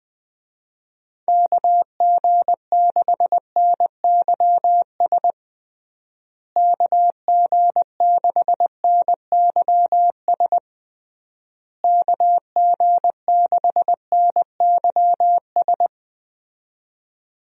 Morse code, heard as a keyed tone, about 21 wpm, 700 Hz. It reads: KG6NYS KG6NYS KG6NYS
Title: USA Call Signs - 6 Characters - Repeated 3x - 20wpm